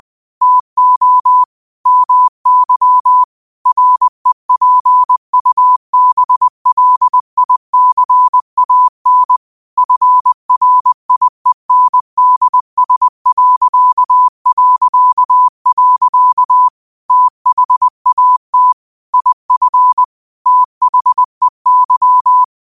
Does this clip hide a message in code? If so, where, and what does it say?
Yes: Morse code from 0.4 to 22.4 s, heard as a keyed tone, about 20 wpm, 1000 Hz. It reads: TO MY REPUBLICAN FRIENDS... THAT IF THEY